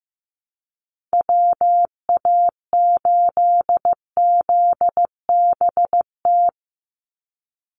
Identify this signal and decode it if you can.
Morse code, heard as a keyed tone, about 15 wpm, 700 Hz. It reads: WA8ZBT
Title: Call Signs Repeated 3x - 15wpm